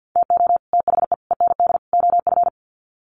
Morse code, heard as a keyed tone, about 50 wpm, 700 Hz. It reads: TO THE END OF
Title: Sets of 4 Words 50wpm